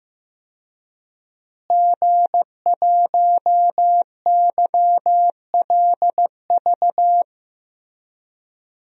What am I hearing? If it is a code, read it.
Morse code, heard as a keyed tone, about 15 wpm, 700 Hz. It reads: G1YLV
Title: DXCC-100 Call Signs - 5 Characters - Repeated 3x - 15wpm